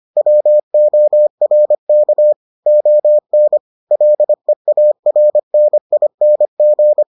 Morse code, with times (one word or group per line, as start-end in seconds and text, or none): 0.2-2.3 WORK
2.7-3.6 ON
3.9-7.0 LEARNING